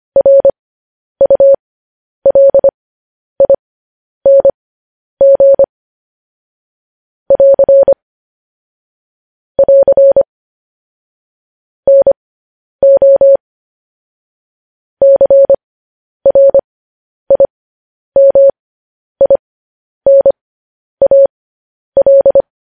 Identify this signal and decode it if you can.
Morse code, heard as a keyed tone, about 25 wpm, spaced as for 10 wpm, 550 Hz. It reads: RULING <AR> <AR> NO CRIMINAL
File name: podcast / News headlines-1x-25x10